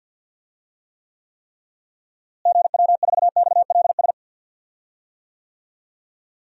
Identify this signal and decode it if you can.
Morse code, heard as a keyed tone, about 48 wpm, 700 Hz. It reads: GW4XLS
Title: DXCC-100 Call Signs - 6 Characters - Repeated 3x - 50wpm